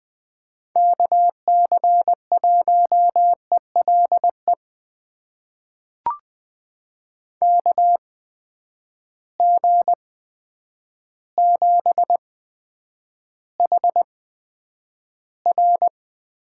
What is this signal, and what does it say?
Morse code, heard as a keyed tone, about 21 wpm, spaced as for 9 wpm, 700 Hz. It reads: KC1ELE KG7HR
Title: USA Call Signs - 6 Characters - 8x Character Spacing - 20wpm